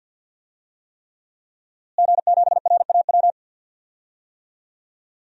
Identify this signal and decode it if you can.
Morse code, heard as a keyed tone, about 49 wpm, 700 Hz. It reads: G7RAW
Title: DXCC-100 Call Signs - 5 Characters - Repeated 3x - 50wpm